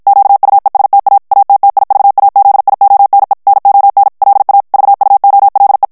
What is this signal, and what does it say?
Morse code, heard as a keyed tone, about 54 wpm, 800 Hz. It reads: QUESTA ATTIVAZIONE NON LA VUOLE
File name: ITALIEN texte clair 1 vitesse 50 mots minute